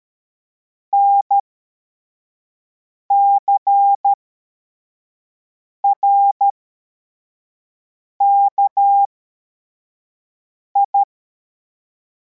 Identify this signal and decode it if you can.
Morse code, heard as a keyed tone, about 13 wpm, spaced as for 4 wpm, 800 Hz. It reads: NCRKI
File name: formation / cw / Text34